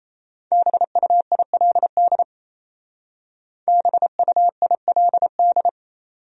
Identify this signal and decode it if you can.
Morse code, heard as a keyed tone, about 34 wpm, 700 Hz. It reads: BUILD BUILD
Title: Speed-Racing Top 500 Words 28wpm